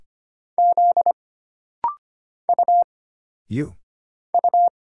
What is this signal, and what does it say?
Morse code, heard as a keyed tone, about 25 wpm, 700 Hz. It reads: Z U U
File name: Single Character (Letter:Number:Punctuation:Prosign) - Rapid-Fire - 25wpm